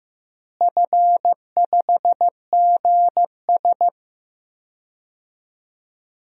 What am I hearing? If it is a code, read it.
Morse code, heard as a keyed tone, about 15 wpm, 700 Hz. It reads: F5GS
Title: DXCC-100 Call Signs - 4 Characters - Repeated 2x - 15wpm